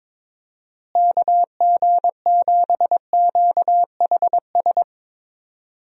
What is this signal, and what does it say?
Morse code, heard as a keyed tone, about 22 wpm, 700 Hz. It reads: KG7QHS
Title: USA Call Signs - 6 Characters - Repeated 2x - 22wpm